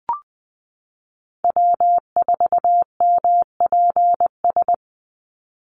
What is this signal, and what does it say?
Morse code, heard as a keyed tone, about 20 wpm, 700 Hz. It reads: W4MPS
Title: NA SOTA Chaser Callsigns 20wpm